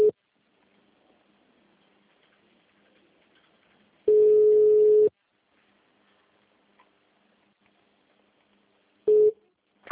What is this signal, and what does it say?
Ringback tone (Europe, 425 Hz), with speech.
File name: out- -2012- - - .5983